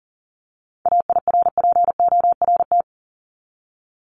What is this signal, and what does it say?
Morse code, heard as a keyed tone, about 40 wpm, 700 Hz. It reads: AIRPORT
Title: Top 1751-2000 Words 40wpm